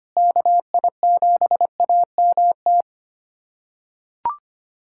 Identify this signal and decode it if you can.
Morse code, heard as a keyed tone, about 25 wpm, 700 Hz. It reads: KI7AMT
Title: USA Callsign DE Callsign - 6 Characters - 25wpm